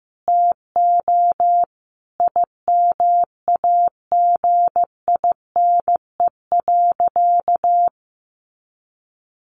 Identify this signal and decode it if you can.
Morse code, heard as a keyed tone, about 15 wpm, 700 Hz. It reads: TO IMAGINE.
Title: Sentences from Top 1400 Words 15wpm